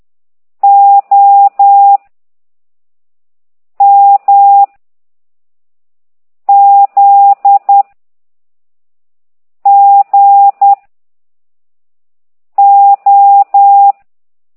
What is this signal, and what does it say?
Morse code, heard as a keyed tone, about 10 wpm, 800 Hz. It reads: O M Z G O